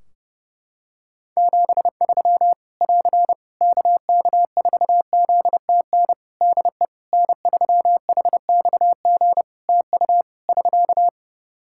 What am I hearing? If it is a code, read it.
Morse code, heard as a keyed tone, about 30 wpm, 700 Hz. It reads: 73 <AR> KK4ZTN DE N3HXG TU <SK>
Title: Practice QSO - Exchange 4 - Ending - 30wpm